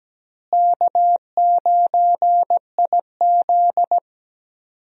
Morse code, heard as a keyed tone, about 17 wpm, 700 Hz. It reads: K9IZ
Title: USA Call Signs - 4 Characters - Repeated 2x - 17wpm